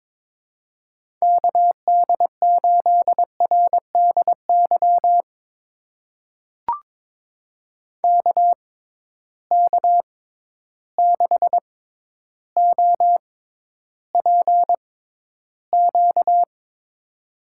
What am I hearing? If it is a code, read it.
Morse code, heard as a keyed tone, about 22 wpm, spaced as for 11 wpm, 700 Hz. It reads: KD8RDY KK6OPQ
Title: USA Call Signs - 6 Characters - 6x Character Spacing - 22wpm